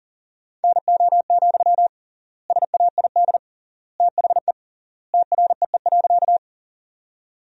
Morse code, heard as a keyed tone, about 37 wpm, 700 Hz. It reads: NO, SAID THE TREE.
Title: 3x Word-Spacing - Sentences from Top 300 Words - 40wpm